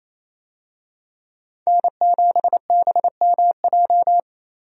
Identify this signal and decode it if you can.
Morse code, heard as a keyed tone, about 28 wpm, 700 Hz. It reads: N7BMJ
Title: USA Call Signs - 5 Characters - 28wpm